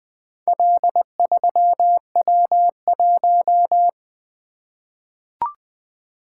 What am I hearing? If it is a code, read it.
Morse code, heard as a keyed tone, about 20 wpm, 700 Hz. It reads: L3W1
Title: Four Letter-Number Combinations - 20wpm